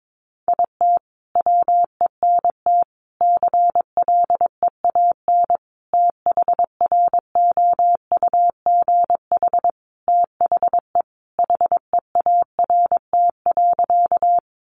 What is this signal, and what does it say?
Morse code, heard as a keyed tone, about 22 wpm, 700 Hz. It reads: IT WENT CLEAN THROUGH THE HEART.
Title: Sentences from Top 1200 Words 22wpm